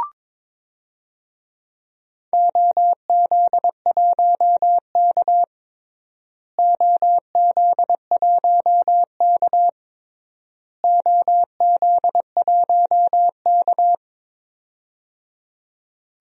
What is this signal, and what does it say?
Morse code, heard as a keyed tone, about 22 wpm, 700 Hz. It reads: OZ1K OZ1K OZ1K
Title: DXCC-100 Call Signs - 4 Characters - Repeated 3x - 22wpm